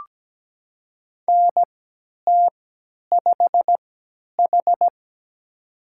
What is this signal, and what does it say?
Morse code, heard as a keyed tone, about 17 wpm, spaced as for 10 wpm, 700 Hz. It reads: NT5H
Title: USA Call Signs - 4 Characters - 3x Character Spacing - 17wpm